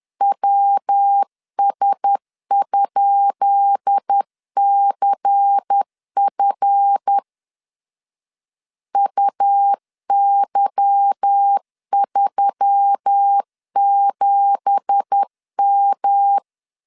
Morse code, heard as a keyed tone, about 10 wpm, 800 Hz. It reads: WS?CF UY37M
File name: LEZIONI / 10 SIGNS / 10B INT 10wpm 2844